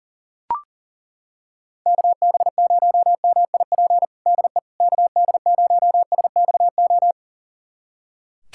Morse code, heard as a keyed tone, about 40 wpm, 700 Hz. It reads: KB0MIP DE KD0SXO
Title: USA Callsign DE Callsign 40wpm